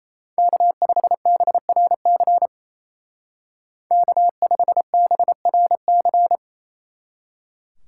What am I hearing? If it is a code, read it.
Morse code, heard as a keyed tone, about 34 wpm, 700 Hz. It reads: K5BRC K5BRC
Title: Speed Racing - USA 1x3 Call Signs 28wpm